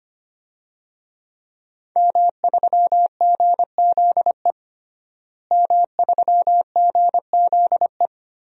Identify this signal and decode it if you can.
Morse code, heard as a keyed tone, about 25 wpm, 700 Hz. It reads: M3GZE M3GZE
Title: DXCC-100 Call Signs - 5 Characters - Repeated 2x - 25wpm